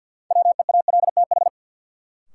Morse code, heard as a keyed tone, about 47 wpm, 700 Hz. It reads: WEALTH
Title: Sentences from Top 1200 Words - New Word Drill Alt - 50wpm